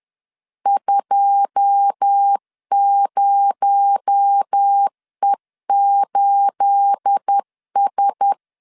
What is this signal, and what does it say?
Morse code, heard as a keyed tone, about 10 wpm, 800 Hz. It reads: 20E8S
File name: CW / DET / 1 9 1500C S B 10wpm 4439